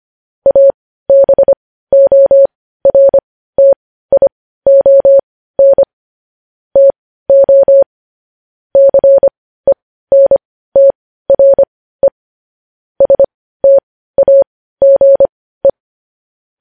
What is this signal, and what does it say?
Morse code, heard as a keyed tone, about 25 wpm, spaced as for 15 wpm, 550 Hz. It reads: ABORTION TO CENTRE STAGE